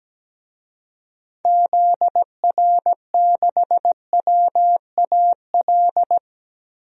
Morse code, heard as a keyed tone, about 17 wpm, 700 Hz. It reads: ZR6WAL
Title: DXCC-100 Call Signs - 6 Characters - Repeated 3x - 17wpm